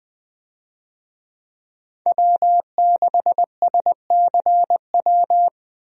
Morse code, heard as a keyed tone, about 20 wpm, 700 Hz. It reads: W6SCW
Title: USA Call Signs - 5 Characters - Repeated 2x - 20wpm